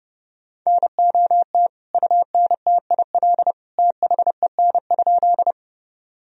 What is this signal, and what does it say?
Morse code, heard as a keyed tone, about 30 wpm, 700 Hz. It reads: NOT UNTIL THEN?
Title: Sentences from Top 1000 Words - Review Part 1 - 30wpm